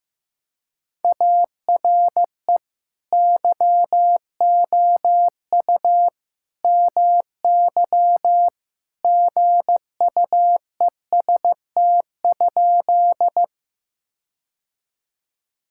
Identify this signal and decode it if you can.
Morse code, heard as a keyed tone, about 15 wpm, 700 Hz. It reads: ARE YOU MY GUEST?
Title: Sentences from Top 1600 Words 15wpm